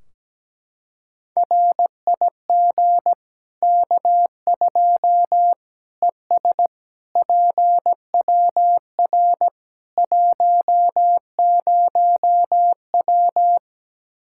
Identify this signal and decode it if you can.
Morse code, heard as a keyed tone, about 17 wpm, 700 Hz. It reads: RIG K2 ES PWR 10W
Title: Practice QSO - Exchange 2 - Info - 17wpm